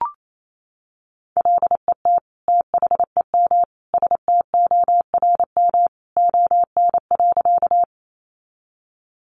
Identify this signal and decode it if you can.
Morse code, heard as a keyed tone, about 28 wpm, 700 Hz. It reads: LET THEM STORM ON.
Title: Sentences from Top 1500 Words 28wpm